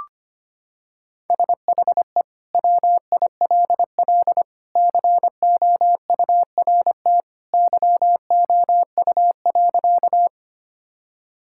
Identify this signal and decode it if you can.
Morse code, heard as a keyed tone, about 25 wpm, 700 Hz. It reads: SHE WILL COURT YOU.
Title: Sentences from Top 1600 Words 25wpm